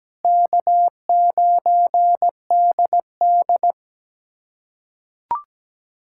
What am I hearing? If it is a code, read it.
Morse code, heard as a keyed tone, about 17 wpm, 700 Hz. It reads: K9DD
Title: USA Callsign DE Callsign - 4 Characters - 17wpm